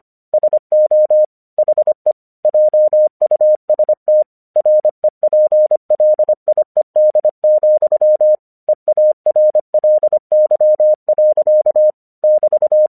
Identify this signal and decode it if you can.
Morse code, heard as a keyed tone, about 25 wpm, 610 Hz. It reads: SO HE JUST REPLIED, EARLY. <BT>
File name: 1 English 25wpm / AQC35